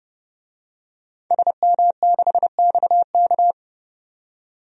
Morse code, heard as a keyed tone, about 32 wpm, 700 Hz. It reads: SM6XK
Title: DXCC-100 Call Signs - 5 Characters - 30wpm